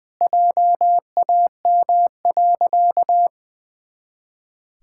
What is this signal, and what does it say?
Morse code, heard as a keyed tone, about 20 wpm, 700 Hz. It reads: JAM.
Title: Pangram Challenge v2 - 20wpm